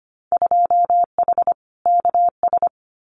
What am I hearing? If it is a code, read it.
Morse code, heard as a keyed tone, about 25 wpm, 700 Hz. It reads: 2H KS
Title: Winter Field Day Exchanges Encore 25wpm